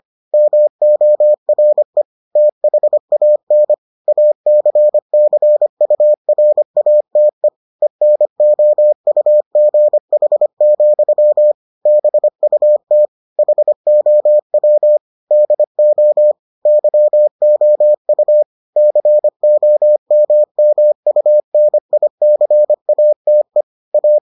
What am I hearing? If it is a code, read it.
Morse code, heard as a keyed tone, about 25 wpm, 600 Hz. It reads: MORE THAN ACCURATE ENOUGH, BUT HOW DO YOU COMMUNICATE A